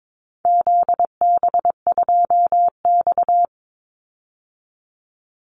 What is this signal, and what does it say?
Morse code, heard as a keyed tone, about 22 wpm, 700 Hz. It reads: ZB2X
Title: Call Signs Repeated 2x - 22wpm